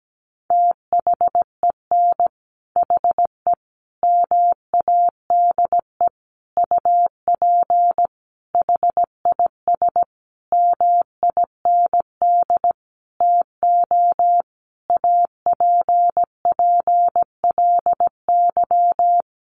Morse code, heard as a keyed tone, about 17 wpm, 700 Hz. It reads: THEN HE MADE UP HIS MIND TO APPLY
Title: Sentences from Top 1400 Words 17wpm